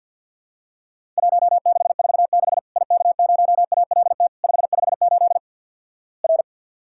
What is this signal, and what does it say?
Morse code, heard as a keyed tone, about 53 wpm, 700 Hz. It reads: 1646 IK0ALT 558 R
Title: Worked All Europe DX Contest - QTC Exchange - 50wpm